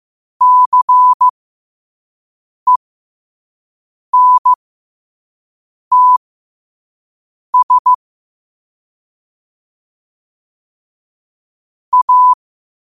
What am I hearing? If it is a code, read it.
Morse code, heard as a keyed tone, about 15 wpm, spaced as for 5 wpm, 1000 Hz. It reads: CENTS A